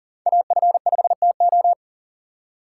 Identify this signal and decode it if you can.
Morse code, heard as a keyed tone, about 51 wpm, 700 Hz. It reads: AF5TO